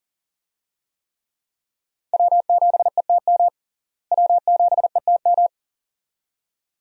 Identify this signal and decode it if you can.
Morse code, heard as a keyed tone, about 40 wpm, 700 Hz. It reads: W7ETM W7ETM
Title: USA Call Signs - 5 Characters - Repeated 2x - 40wpm